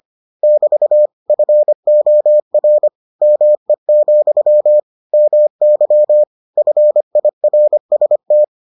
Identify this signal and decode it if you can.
Morse code, heard as a keyed tone, about 25 wpm, 600 Hz. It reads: <BT> FOR ME, MY FIRST